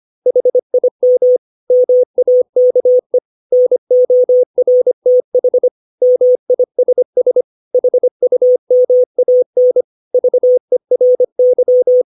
Morse code, heard as a keyed tone, about 25 wpm, 500 Hz. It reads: HIM MAKE NORTH MISS HUMAN VERY
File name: 632 words / 632 Words 25WPM